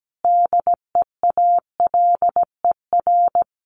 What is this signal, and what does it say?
Morse code, heard as a keyed tone, about 17 wpm, 700 Hz. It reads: DEALER
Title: Sentences from Top 1100 Words 17wpm